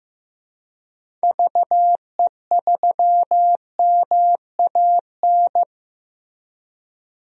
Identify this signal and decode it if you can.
Morse code, heard as a keyed tone, about 15 wpm, 700 Hz. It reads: VE3MAN
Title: DXCC-100 Call Signs - 6 Characters - Repeated 2x - 15wpm